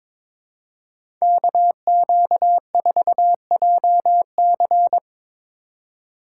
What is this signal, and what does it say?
Morse code, heard as a keyed tone, about 22 wpm, 700 Hz. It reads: KQ4JC
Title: USA Call Signs - 2x Character Spacing - 22wpm